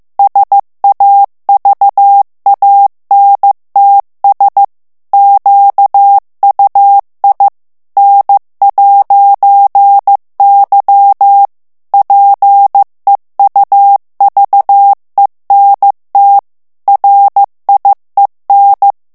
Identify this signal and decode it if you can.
Morse code, heard as a keyed tone, about 15 wpm, 800 Hz. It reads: SAVANTS QUI N'Y PEUVENT RIEN
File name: FRANCAIS TEXTE CLAIR 2 VITESSE 14 MOTS MINUTES